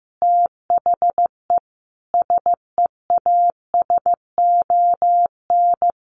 Morse code, heard as a keyed tone, about 15 wpm, 700 Hz. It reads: THE SEASON
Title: Sets of 5 Words Encore 15wpm